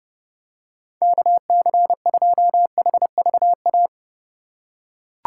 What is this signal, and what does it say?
Morse code, heard as a keyed tone, about 30 wpm, 700 Hz. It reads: KC2HVA
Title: USA Call Signs - 6 Characters - 4x Character Spacing - 30wpm